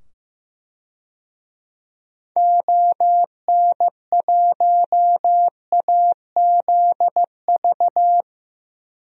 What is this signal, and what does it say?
Morse code, heard as a keyed tone, about 15 wpm, 700 Hz. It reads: ON1AZV
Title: DXCC-100 Call Signs - 6 Characters - 15wpm